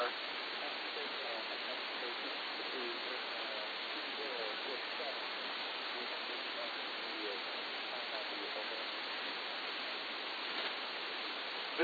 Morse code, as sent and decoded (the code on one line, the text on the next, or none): none